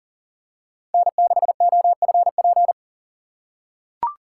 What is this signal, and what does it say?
Morse code, heard as a keyed tone, about 39 wpm, 700 Hz. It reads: N6OFP
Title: USA Call Signs - 5 Characters - 8x Character Spacing - 40wpm